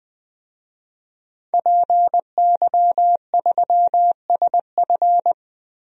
Morse code, heard as a keyed tone, about 20 wpm, 700 Hz. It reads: PY3SF
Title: DXCC-100 Call Signs - 5 Characters - Repeated 2x - 20wpm